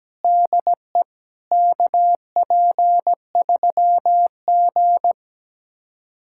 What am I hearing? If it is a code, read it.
Morse code, heard as a keyed tone, about 17 wpm, 700 Hz. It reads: DE KP3G
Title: USA Callsign DE Callsign - 4 Characters - 17wpm